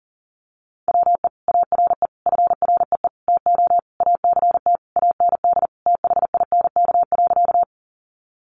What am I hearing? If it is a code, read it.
Morse code, heard as a keyed tone, about 40 wpm, 700 Hz. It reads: WE ARE FREE TO ACT AND THINK.
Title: Sentences from Top 500 Words 40wpm